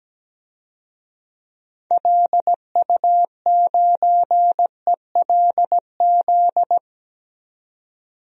Morse code, heard as a keyed tone, about 17 wpm, 700 Hz. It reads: LU9ELZ
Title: DXCC-100 Call Signs - 6 Characters - 17wpm